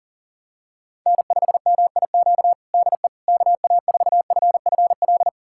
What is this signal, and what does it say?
Morse code, heard as a keyed tone, about 39 wpm, 700 Hz. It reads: N5MIQ DE KA4FFL
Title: USA Callsign DE Callsign 40wpm